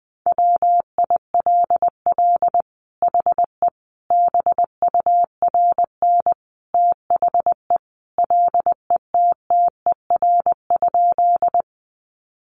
Morse code, heard as a keyed tone, about 20 wpm, 700 Hz. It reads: WILL HE BURN THE LETTER?
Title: Sentences from Top 1300 Words 20wpm